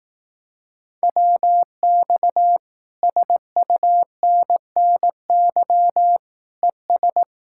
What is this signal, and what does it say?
Morse code, heard as a keyed tone, about 18 wpm, 700 Hz. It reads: WX SUNNY ES
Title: Speed-Racing Common QSO Phrases Encore - 15wpm